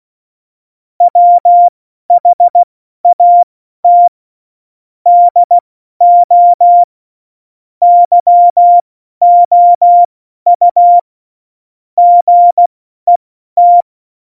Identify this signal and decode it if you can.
Morse code, heard as a keyed tone, about 16 wpm, spaced as for 12 wpm, 700 Hz. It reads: WHAT DO YOU GET
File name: Morsekurs / Witze